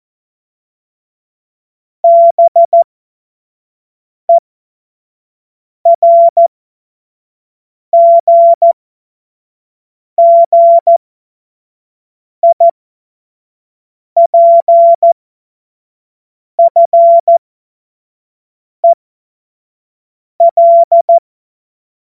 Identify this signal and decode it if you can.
Morse code, heard as a keyed tone, about 14 wpm, spaced as for 5 wpm, 680 Hz. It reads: BERGGIPFEL
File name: Witz-09-September-2025-25-F70